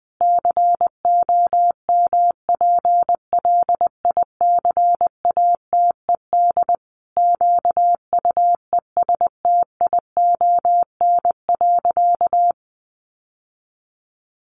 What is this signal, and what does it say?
Morse code, heard as a keyed tone, about 20 wpm, 700 Hz. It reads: COMPLICATED QUESTION.
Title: Sentences from Top 1200 Words 20wpm